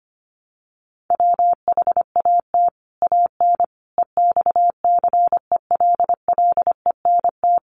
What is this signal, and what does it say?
Morse code, heard as a keyed tone, about 25 wpm, 700 Hz. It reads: WHAT AN EXCELLENT
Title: Sentences from Top 1000 Words 25wpm